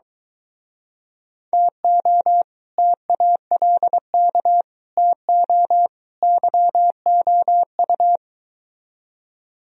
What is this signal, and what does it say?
Morse code, heard as a keyed tone, about 22 wpm, 700 Hz. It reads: TO TALK TO YOU